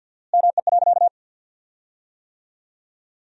Morse code, heard as a keyed tone, about 53 wpm, 700 Hz. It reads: ME.
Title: 4x Word-Spacing - Sentences from Top 500 Words - 50wpm